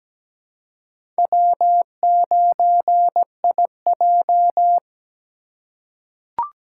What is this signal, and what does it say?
Morse code, heard as a keyed tone, about 17 wpm, 700 Hz. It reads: W9IJ
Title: USA Call Signs - 2x Character Spacing - 17wpm